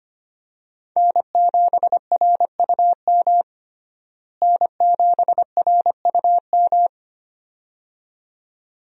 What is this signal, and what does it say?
Morse code, heard as a keyed tone, about 25 wpm, 700 Hz. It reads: N7RUM N7RUM
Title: USA Call Signs - 5 Characters - Repeated 2x - 25wpm